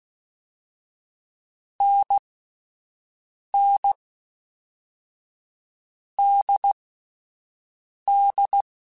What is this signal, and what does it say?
Morse code, heard as a keyed tone, about 16 wpm, 800 Hz. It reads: N N D D
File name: Difficult Combos / Difficult Combos-mp3s / awj1 ndb6 gz7 @10